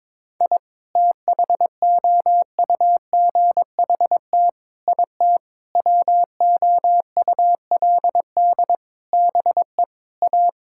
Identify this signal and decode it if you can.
Morse code, heard as a keyed tone, about 22 wpm, 700 Hz. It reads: I THOUGHT IT WOULD BE A
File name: Sets of 9 Words - 22wpm